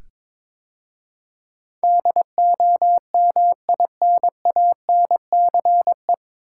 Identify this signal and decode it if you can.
Morse code, heard as a keyed tone, about 22 wpm, 700 Hz. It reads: DOMINANCE
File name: 9 Letter Words - First 3 Letters Given - 22wpm